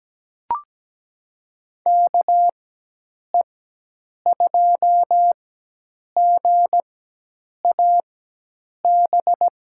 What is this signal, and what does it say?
Morse code, heard as a keyed tone, about 17 wpm, spaced as for 8 wpm, 700 Hz. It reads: KE2GAB
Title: USA Call Signs - 6 Characters - 4x Character Spacing - 17wpm